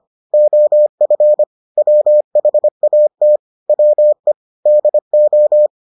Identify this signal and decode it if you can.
Morse code, heard as a keyed tone, about 25 wpm, 600 Hz. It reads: OF WHAT WE DO